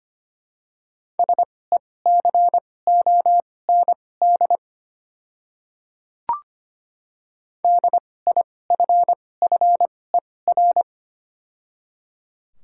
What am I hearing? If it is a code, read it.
Morse code, heard as a keyed tone, about 25 wpm, spaced as for 18 wpm, 700 Hz. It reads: SECOND DIFFER
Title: 2x Character Spacing - Top 300 Words - 25wpm